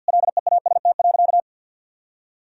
Morse code, heard as a keyed tone, about 52 wpm, 700 Hz. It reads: LEAST.
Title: Sentences from Top 1000 Words - Review Part 2 - 50wpm